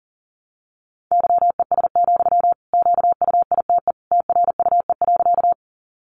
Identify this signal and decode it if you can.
Morse code, heard as a keyed tone, about 40 wpm, 700 Hz. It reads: YES, QUITE TRUE.
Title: Sentences from Top 700 Words 40wpm